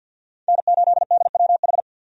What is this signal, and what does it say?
Morse code, heard as a keyed tone, about 52 wpm, 700 Hz. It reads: N8DWH
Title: USA Call Signs - 5 Characters - 50wpm